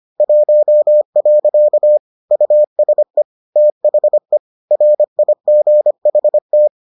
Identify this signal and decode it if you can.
Morse code, heard as a keyed tone, about 25 wpm, 600 Hz. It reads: 1. USE THE RIGHT